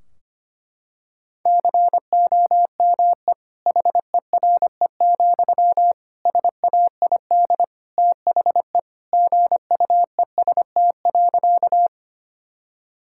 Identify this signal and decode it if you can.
Morse code, heard as a keyed tone, about 25 wpm, 700 Hz. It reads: COME HERE, SAID THE GUEST.
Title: Sentences from Top 1600 Words 25wpm